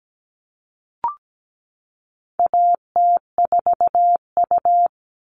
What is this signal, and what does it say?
Morse code, heard as a keyed tone, about 17 wpm, 700 Hz. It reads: AT4U
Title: All Cut Numbers - 4-Digit 17wpm